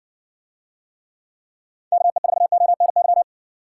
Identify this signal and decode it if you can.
Morse code, heard as a keyed tone, about 60 wpm, 700 Hz. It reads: KE4ONY